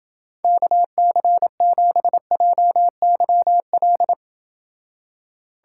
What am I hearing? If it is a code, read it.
Morse code, heard as a keyed tone, about 29 wpm, 700 Hz. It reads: KC7JYL